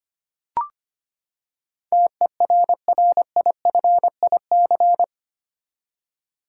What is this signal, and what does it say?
Morse code, heard as a keyed tone, about 25 wpm, 700 Hz. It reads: TERRIFIC
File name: Eight Letter Words - 25wpm